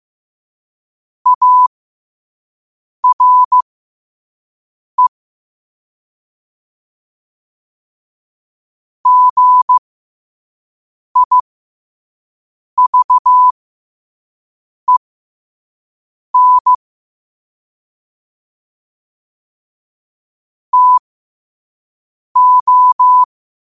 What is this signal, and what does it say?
Morse code, heard as a keyed tone, about 15 wpm, spaced as for 5 wpm, 1000 Hz. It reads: ARE GIVEN TO
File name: qotdfiles / 5wpm